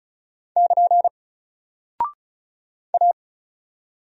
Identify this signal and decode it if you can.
Morse code, heard as a keyed tone, about 34 wpm, 700 Hz. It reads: <KN> A
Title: Single Character (Letter:Number:Punctuation:Prosign) Alt 35wpm